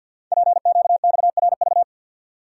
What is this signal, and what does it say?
Morse code, heard as a keyed tone, about 48 wpm, 690 Hz. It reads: PQXRV
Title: Code Groups - 5 Letters - 50wpm